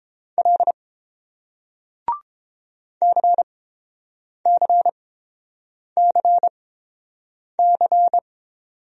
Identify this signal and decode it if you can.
Morse code, heard as a keyed tone, about 28 wpm, spaced as for 7 wpm, 690 Hz. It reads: L CCCC